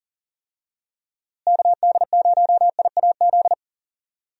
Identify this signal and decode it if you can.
Morse code, heard as a keyed tone, about 40 wpm, 700 Hz. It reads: KD0IAZ
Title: USA Call Signs - 6 Characters - Repeated 2x - 40wpm